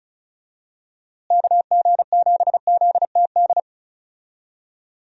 Morse code, heard as a keyed tone, about 34 wpm, 700 Hz. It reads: KG7ZTD
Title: USA Call Signs - 6 Characters - Repeated 3x - 35wpm